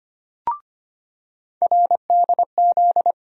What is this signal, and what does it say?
Morse code, heard as a keyed tone, about 25 wpm, 700 Hz. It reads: RDZ
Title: USA Call Sign Postfixes - 3 Letters 25wpm